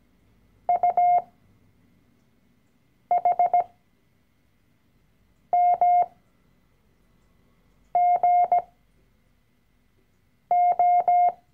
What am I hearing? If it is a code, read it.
Morse code, heard as a keyed tone, about 17 wpm, spaced as for 4 wpm, 710 Hz. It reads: UHMGO
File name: CW-Online-Kurs / Anfänger / Lektion 012